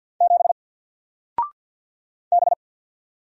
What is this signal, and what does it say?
Morse code, heard as a keyed tone, about 48 wpm, 700 Hz. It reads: 7 B
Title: Single Character (Letter:Number:Punctuation:Prosign) 50wpm